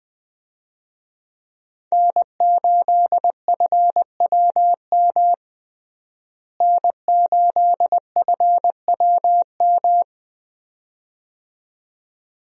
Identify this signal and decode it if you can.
Morse code, heard as a keyed tone, about 20 wpm, 700 Hz. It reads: N8FWM N8FWM
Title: USA Call Signs - 5 Characters - Repeated 2x - 20wpm